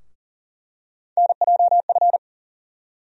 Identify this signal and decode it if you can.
Morse code, heard as a keyed tone, about 40 wpm, 700 Hz. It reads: NJF
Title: USA Call Sign Postfixes - 3 Letters 40wpm